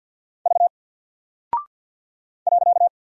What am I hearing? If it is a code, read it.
Morse code, heard as a keyed tone, about 48 wpm, 700 Hz. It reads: V .
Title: Single Character (Letter:Number:Punctuation:Prosign) Alt 50wpm